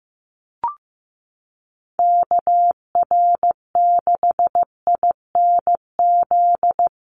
Morse code, heard as a keyed tone, about 15 wpm, 700 Hz. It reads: KR6INZ
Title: USA Call Signs Repeated 2x 15wpm